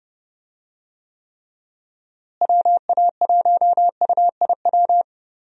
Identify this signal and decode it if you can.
Morse code, heard as a keyed tone, about 30 wpm, 700 Hz. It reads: WA1UIW
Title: USA Call Signs - 6 Characters - Repeated 3x - 30wpm